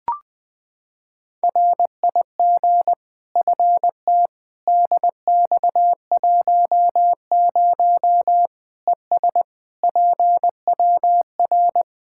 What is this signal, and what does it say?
Morse code, heard as a keyed tone, about 20 wpm, 700 Hz. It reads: RIG FT DX10 ES PWR
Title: Practice QSO - Exchange 2 - Info - 20wpm